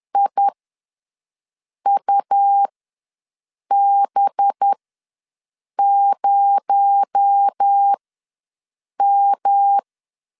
Morse code, heard as a keyed tone, about 10 wpm, spaced as for 6 wpm, 800 Hz. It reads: IUB0M
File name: LEZIONI / 9 2867 ws 10wpm 4215